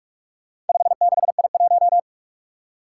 Morse code, heard as a keyed tone, about 46 wpm, 700 Hz. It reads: 56I1
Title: Four Challenging Letter-Number Combinations 45wpm